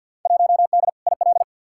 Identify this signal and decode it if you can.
Morse code, heard as a keyed tone, about 52 wpm, 700 Hz. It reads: 1D IL
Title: Field Day Exchanges - 50wpm